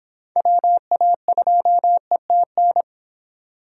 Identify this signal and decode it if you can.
Morse code, heard as a keyed tone, about 26 wpm, 700 Hz. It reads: WA2ETN